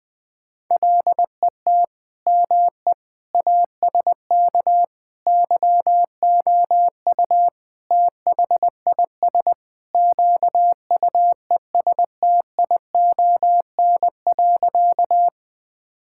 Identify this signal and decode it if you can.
Morse code, heard as a keyed tone, about 21 wpm, 700 Hz. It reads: LET ME ASK YOU THIS QUESTION.
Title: Sentences with 5-Word Phrases - 20wpm